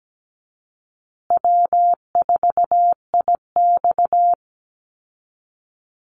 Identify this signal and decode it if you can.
Morse code, heard as a keyed tone, about 17 wpm, 700 Hz. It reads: W4IX
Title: Call Signs Repeated 2x - 17wpm